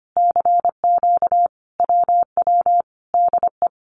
Morse code, heard as a keyed tone, about 25 wpm, 700 Hz. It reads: CQ WW DE
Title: Common QSO Phrases Encore 25wpm